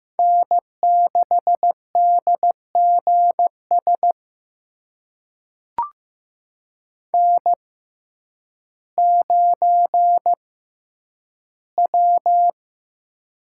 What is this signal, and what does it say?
Morse code, heard as a keyed tone, about 15 wpm, spaced as for 9 wpm, 700 Hz. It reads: N6DGS N9W
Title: USA Call Signs - 5 Characters - 6x Character Spacing - 15wpm